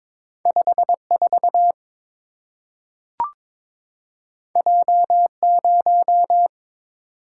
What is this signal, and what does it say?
Morse code, heard as a keyed tone, about 22 wpm, 700 Hz. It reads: 54 J0
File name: practice / 22wpm / Two Letter-Number Combinations - 22wpm